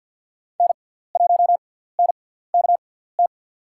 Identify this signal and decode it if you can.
Morse code, heard as a keyed tone, about 47 wpm, spaced as for 17 wpm, 700 Hz. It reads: N1NKT
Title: USA Call Signs - 6x Character Spacing - 50wpm